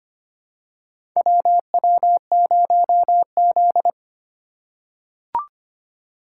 Morse code, heard as a keyed tone, about 25 wpm, 700 Hz. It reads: WW0Z
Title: USA Call Signs - 4 Characters - 2x Character Spacing - 25wpm